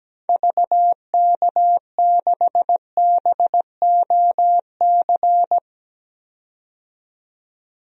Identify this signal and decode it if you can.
Morse code, heard as a keyed tone, about 17 wpm, 700 Hz. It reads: VK6BOC
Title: DXCC-100 Call Signs - 6 Characters - 17wpm